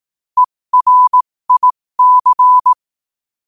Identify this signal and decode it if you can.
Morse code, heard as a keyed tone, about 18 wpm, spaced as for 15 wpm, 1000 Hz. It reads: ERIC
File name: qotdfiles / 15wpm